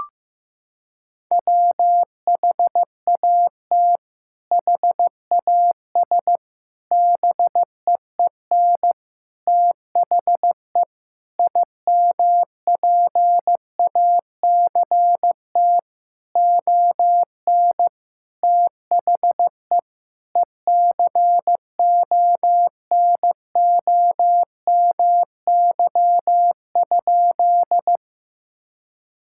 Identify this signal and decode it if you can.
Morse code, heard as a keyed tone, about 15 wpm, 700 Hz. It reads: WHAT HAS BEEN THE IMPACT ON THE ECONOMY?
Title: Sentences from Top 1000 Words - Review Part 6 - 15wpm